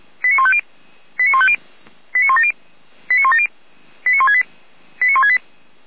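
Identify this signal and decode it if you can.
Five-tone selective call (ZVEI), heard as quick runs of five tones, tones of about 70 ms. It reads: 89150 89150 89170 89170 89168 89168
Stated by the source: ZVEI five-tone selective call received with an SDR.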